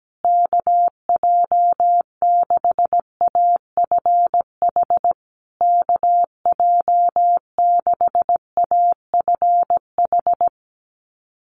Morse code, heard as a keyed tone, about 17 wpm, 700 Hz. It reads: KJ6AFH KJ6AFH
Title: USA Call Signs Repeated 3x 17wpm